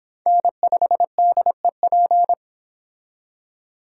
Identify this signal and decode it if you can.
Morse code, heard as a keyed tone, about 25 wpm, 700 Hz. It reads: N5DEP